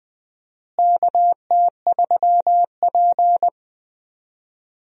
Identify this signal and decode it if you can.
Morse code, heard as a keyed tone, about 20 wpm, 700 Hz. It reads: KT3P
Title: USA Call Signs - 4 Characters - Repeated 2x - 20wpm